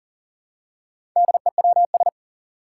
Morse code, heard as a keyed tone, about 40 wpm, 700 Hz. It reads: DEWS
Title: Four Challenging Letter-Number Combinations 40wpm